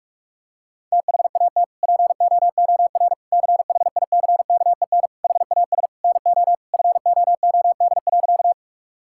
Morse code, heard as a keyed tone, about 46 wpm, 700 Hz. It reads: THAT POOR CHICKEN HAS NO FOOD.
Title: Sentences from Top 1000 Words - Review Part 5 - 45wpm